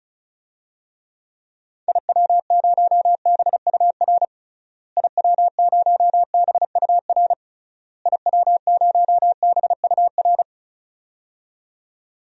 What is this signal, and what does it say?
Morse code, heard as a keyed tone, about 36 wpm, 700 Hz. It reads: IW0BUR IW0BUR IW0BUR
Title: DXCC-100 Call Signs - 6 Characters - Repeated 3x - 35wpm